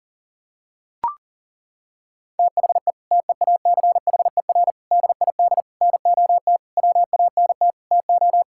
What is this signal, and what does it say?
Morse code, heard as a keyed tone, about 37 wpm, 700 Hz. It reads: THE TEACHER DID NOT WANT TO
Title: Sentences from Top 1000 Words - Review Part 6 - 40wpm